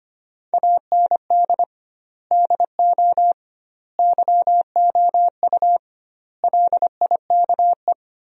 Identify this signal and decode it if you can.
Morse code, heard as a keyed tone, about 25 wpm, 700 Hz. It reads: AND DO YOU LIKE
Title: 2x Word-Spacing - Sentences from Top 100 Words - 25wpm